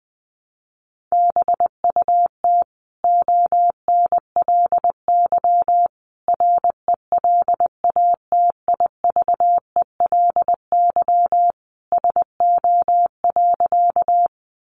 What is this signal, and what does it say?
Morse code, heard as a keyed tone, about 19 wpm, 700 Hz. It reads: BUT ONLY RELATIVELY SO.
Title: Sentences from Top 1100 Words 20wpm